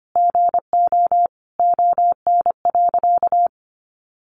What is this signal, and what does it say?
Morse code, heard as a keyed tone, about 25 wpm, 700 Hz. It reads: GO ON.
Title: Sentences from Top 500 Words 25wpm